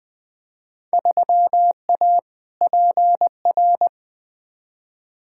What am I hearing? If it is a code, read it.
Morse code, heard as a keyed tone, about 21 wpm, 700 Hz. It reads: 3A PR
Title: Field Day Exchanges Encore - 20wpm